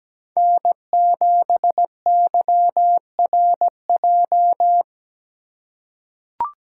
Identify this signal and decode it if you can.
Morse code, heard as a keyed tone, about 17 wpm, 700 Hz. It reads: N7YRJ
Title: USA Call Signs - 5 Characters - 2x Character Spacing - 17wpm